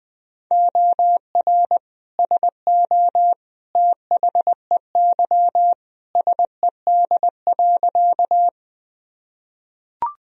Morse code, heard as a keyed tone, about 20 wpm, 700 Hz. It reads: OR SO THEY SED.
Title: Sentences from Top 100 Words and Abbreviations 20wpm